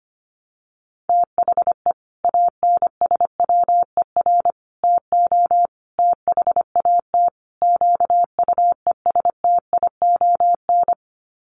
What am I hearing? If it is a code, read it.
Morse code, heard as a keyed tone, about 25 wpm, 700 Hz. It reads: THE ANSWER TO THAT QUESTION
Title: Sets of 5 Words Encore2 25wpm